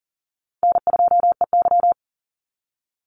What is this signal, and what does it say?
Morse code, heard as a keyed tone, about 40 wpm, 700 Hz. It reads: N2EY
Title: Easy Call Signs - 40wpm